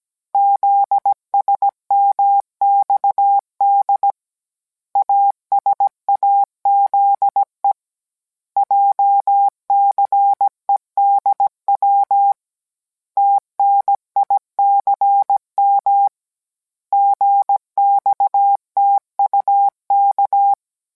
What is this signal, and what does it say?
Morse code, heard as a keyed tone, about 17 wpm, 800 Hz. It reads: ZSMXD ASAZE JCEDW TNICM GXTUK